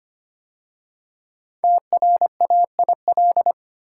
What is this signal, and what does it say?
Morse code, heard as a keyed tone, about 25 wpm, 700 Hz. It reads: TRAIL
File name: 5 Letter Words - First 3 Letters Given - 25wpm